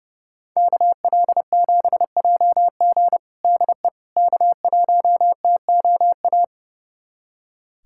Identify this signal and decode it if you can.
Morse code, heard as a keyed tone, about 30 wpm, 700 Hz. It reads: KL7JG DE K1TOA
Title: USA Callsign DE Callsign - 5 Characters - 30wpm